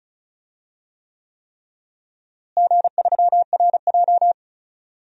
Morse code, heard as a keyed tone, about 34 wpm, 700 Hz. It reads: G3RJ
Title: DXCC-100 Call Signs - 4 Characters - Repeated 3x - 35wpm